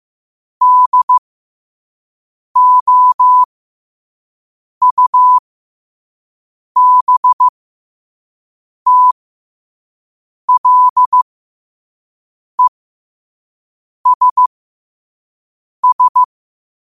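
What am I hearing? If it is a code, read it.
Morse code, heard as a keyed tone, about 15 wpm, spaced as for 5 wpm, 1000 Hz. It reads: DOUBTLESS
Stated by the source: Morse practice at 5 wpm.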